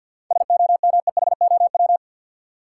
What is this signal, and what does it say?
Morse code, heard as a keyed tone, about 50 wpm, 700 Hz. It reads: SOMEHOW
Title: Seven Letter Words - 50wpm